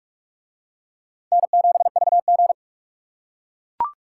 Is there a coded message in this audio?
Yes: Morse code from 1.3 to 2.5 s, heard as a keyed tone, about 44 wpm, 700 Hz. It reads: N7VG